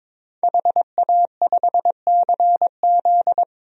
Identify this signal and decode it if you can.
Morse code, heard as a keyed tone, about 22 wpm, 700 Hz. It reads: HA5CZ
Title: DXCC-100 Call Signs - 5 Characters - 22wpm